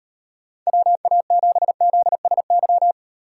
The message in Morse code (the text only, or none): WA7ZSY